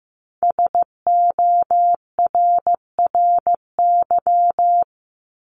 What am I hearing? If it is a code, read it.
Morse code, heard as a keyed tone, about 15 wpm, 700 Hz. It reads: SORRY
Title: Top 1751-2000 Words 15wpm